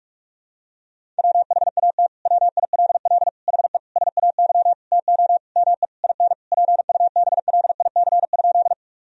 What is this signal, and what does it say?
Morse code, heard as a keyed tone, about 46 wpm, 700 Hz. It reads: WHAT WILL HE SAY TO ME IN PUBLIC?
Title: Sentences from Top 1000 Words - Review Part 3 - 45wpm